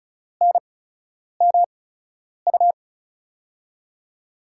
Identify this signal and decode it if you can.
Morse code, heard as a keyed tone, about 36 wpm, spaced as for 10 wpm, 700 Hz. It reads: NMU
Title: USA Call Signs - 8x Character Spacing - 35wpm